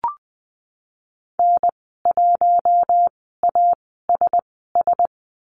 Morse code, heard as a keyed tone, about 20 wpm, spaced as for 15 wpm, 700 Hz. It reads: N1ASS
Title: 076 Lesson 13g - Letter L - Call Signs 20wpm